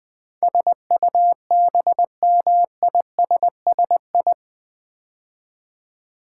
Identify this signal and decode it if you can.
Morse code, heard as a keyed tone, about 20 wpm, 700 Hz. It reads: SUBMISSI
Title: Build A Word - 4 Letters At A Time - 10-Letter Words - 20wpm